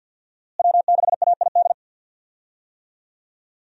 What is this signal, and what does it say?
Morse code, heard as a keyed tone, about 51 wpm, 700 Hz. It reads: W6AID
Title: USA Call Signs - 5 Characters - Repeated 3x - 50wpm